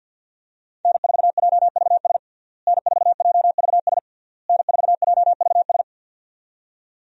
Morse code, heard as a keyed tone, about 47 wpm, 700 Hz. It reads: N4JVS N4JVS N4JVS
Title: USA Call Signs - 5 Characters - Repeated 3x - 50wpm